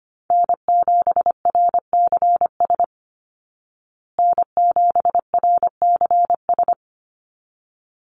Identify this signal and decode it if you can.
Morse code, heard as a keyed tone, about 25 wpm, 700 Hz. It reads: N7RCS N7RCS
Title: Call Signs Repeated 3x - 25wpm